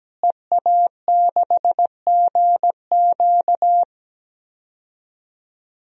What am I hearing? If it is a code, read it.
Morse code, heard as a keyed tone, about 17 wpm, 700 Hz. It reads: EA6GQ
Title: DXCC-100 Call Signs - 5 Characters - Repeated 2x - 17wpm